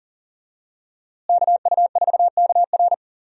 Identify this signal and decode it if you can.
Morse code, heard as a keyed tone, about 40 wpm, 700 Hz. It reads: KU4KR
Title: USA Call Signs - 5 Characters - Repeated 3x - 40wpm